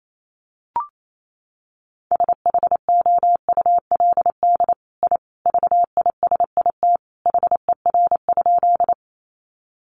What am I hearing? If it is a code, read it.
Morse code, heard as a keyed tone, about 28 wpm, 700 Hz. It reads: SHOULD I VISIT HER?
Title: Sentences from Top 900 Words 28wpm